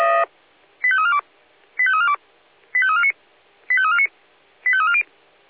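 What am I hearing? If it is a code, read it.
Five-tone selective call (ZVEI), heard as quick runs of five tones, tones of about 70 ms. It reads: 85342 85342 85349 85349 85340
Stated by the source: Received by SDR, ZVEI five-tone call 85357.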